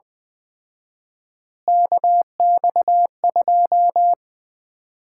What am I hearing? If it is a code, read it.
Morse code, heard as a keyed tone, about 20 wpm, 700 Hz. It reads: KX2
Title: Common QSO Phrases - 4x Word-Spacing - 20wpm